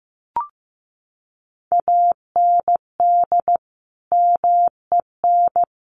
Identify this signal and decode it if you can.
Morse code, heard as a keyed tone, about 15 wpm, 700 Hz. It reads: AND MEN
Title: Sentences from Top 1400 Words 15wpm